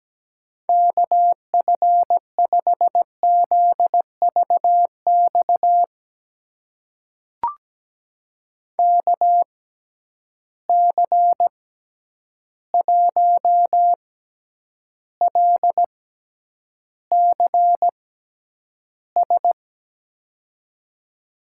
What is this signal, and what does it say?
Morse code, heard as a keyed tone, about 17 wpm, spaced as for 9 wpm, 700 Hz. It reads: KF5ZVX KC1LCS
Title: USA Call Signs - 6 Characters - 6x Character Spacing - 17wpm